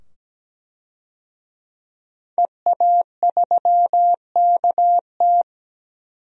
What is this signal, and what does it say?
Morse code, heard as a keyed tone, about 17 wpm, 700 Hz. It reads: EA3KT
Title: DXCC-100 Call Signs - 5 Characters - 17wpm